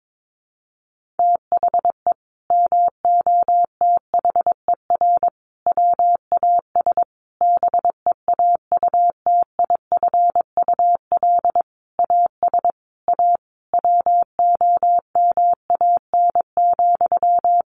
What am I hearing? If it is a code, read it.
Morse code, heard as a keyed tone, about 22 wpm, 700 Hz. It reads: THE MOTHER WAS BEAUTIFUL AS A WOMAN,
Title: Sentences from Top 1500 Words 22wpm